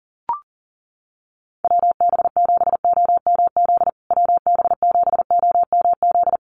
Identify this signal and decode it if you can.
Morse code, heard as a keyed tone, about 40 wpm, 700 Hz. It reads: WB7OMZ WB7OMZ
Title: USA Call Signs Repeated 2x 40wpm